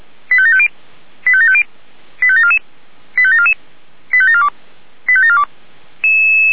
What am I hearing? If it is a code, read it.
Five-tone selective call (ZVEI), heard as quick runs of five tones, tones of about 70 ms. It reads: 85750 85750 85744 85744 85742 85742